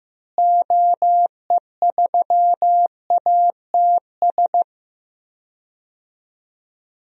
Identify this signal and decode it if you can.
Morse code, heard as a keyed tone, about 15 wpm, 700 Hz. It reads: OE3ATS
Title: DXCC-100 Call Signs - 6 Characters - Repeated 3x - 15wpm